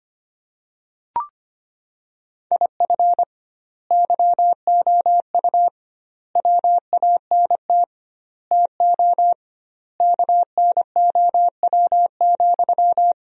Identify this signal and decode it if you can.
Morse code, heard as a keyed tone, about 25 wpm, 700 Hz. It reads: IF YOU WANT TO KNOW,
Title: 2x Word-Spacing - Sentences from Top 200 Words - 25wpm